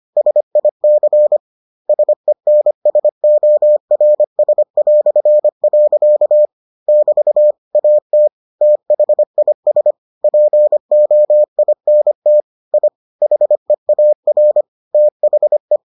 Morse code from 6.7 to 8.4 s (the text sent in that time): <BT> AT